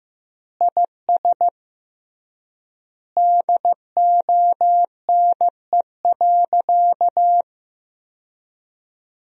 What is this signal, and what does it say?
Morse code, heard as a keyed tone, about 15 wpm, 700 Hz. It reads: IS DONE.
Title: 3x Word-Spacing - Sentences from Top 500 Words - 15wpm